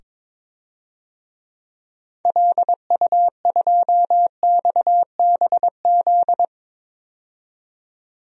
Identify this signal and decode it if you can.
Morse code, heard as a keyed tone, about 22 wpm, 700 Hz. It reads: LU2XBZ
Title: DXCC-100 Call Signs - 6 Characters - 22wpm